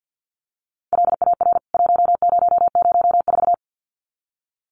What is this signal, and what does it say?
Morse code, heard as a keyed tone, about 50 wpm, 700 Hz. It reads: LAR 1004
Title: CWT Exchange 50wpm